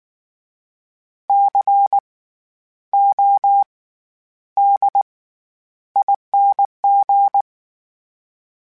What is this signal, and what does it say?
Morse code, heard as a keyed tone, about 19 wpm, spaced as for 10 wpm, 800 Hz. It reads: CODING
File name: Special INGs / Special INGs-mp3s / SlowToFaster ing @18